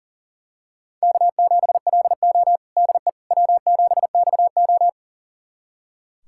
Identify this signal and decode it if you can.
Morse code, heard as a keyed tone, about 40 wpm, 700 Hz. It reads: K7LO DE W7XO
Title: USA Callsign DE Callsign - 4 Characters - 40wpm